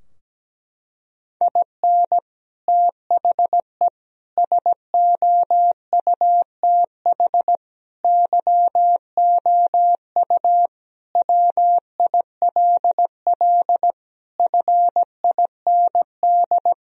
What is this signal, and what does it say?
Morse code, heard as a keyed tone, about 17 wpm, 700 Hz. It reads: IN THE SOUTH YOU WILL FIND
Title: Sentences from Top 1000 Words - Review Part 2 - 17wpm